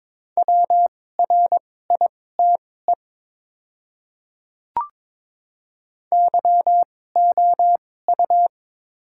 Morse code, heard as a keyed tone, about 22 wpm, spaced as for 16 wpm, 700 Hz. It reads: WRITE YOU
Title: 2x Character Spacing - Top 500 Words - 22wpm